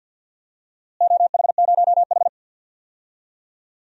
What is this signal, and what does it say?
Morse code, heard as a keyed tone, about 51 wpm, 700 Hz. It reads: OH0H
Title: DXCC-100 Call Signs - 4 Characters - Repeated 2x - 50wpm